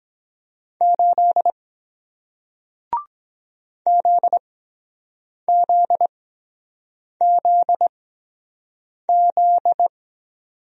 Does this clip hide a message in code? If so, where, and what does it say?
Yes: Morse code from 0.8 to 9.9 s, heard as a keyed tone, about 23 wpm, spaced as for 7 wpm, 700 Hz. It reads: 8 ZZZZ